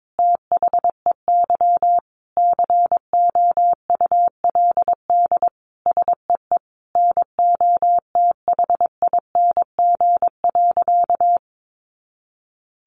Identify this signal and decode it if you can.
Morse code, heard as a keyed tone, about 22 wpm, 700 Hz. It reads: THEY COULD SEE NOTHING.
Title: Sentences from Top 500 Words 22wpm